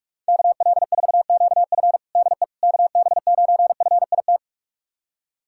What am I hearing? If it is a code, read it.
Morse code, heard as a keyed tone, about 46 wpm, 700 Hz. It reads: KL4QF DE KB9FIT
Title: USA Callsign DE Callsign 45wpm